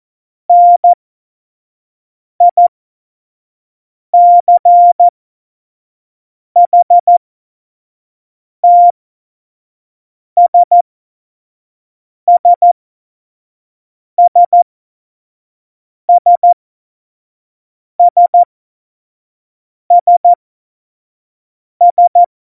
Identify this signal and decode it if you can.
Morse code, heard as a keyed tone, about 14 wpm, spaced as for 5 wpm, 700 Hz. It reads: NICHTSSSSSSS
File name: Morsekurs / Witze / Witz 26. September 17, Tempo 25